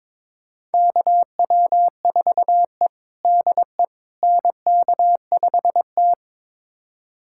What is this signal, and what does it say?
Morse code, heard as a keyed tone, about 22 wpm, 700 Hz. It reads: KW4E DE NK5T
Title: USA Callsign DE Callsign - 4 Characters - 22wpm